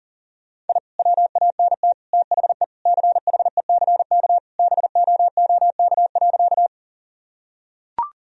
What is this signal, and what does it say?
Morse code, heard as a keyed tone, about 40 wpm, 700 Hz. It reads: I WANT THE CHECK BOOK.
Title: Sentences from Top 1000 Words - Review Part 3 - 40wpm